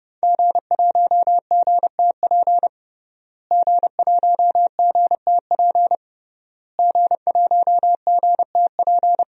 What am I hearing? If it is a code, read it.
Morse code, heard as a keyed tone, about 30 wpm, 700 Hz. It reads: G1GTP G1GTP G1GTP
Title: DXCC-100 Call Signs - 5 Characters - Repeated 3x - 30wpm